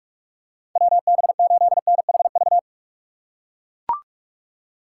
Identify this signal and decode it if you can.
Morse code, heard as a keyed tone, about 43 wpm, 700 Hz. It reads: WB8NHV
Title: USA Call Signs - 6 Characters - 6x Character Spacing - 45wpm